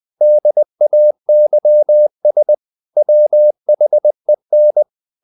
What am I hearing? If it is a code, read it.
Morse code, heard as a keyed tone, about 20 wpm, 600 Hz. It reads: DAYS WHEN
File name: the old man and the sea part 2 20wpm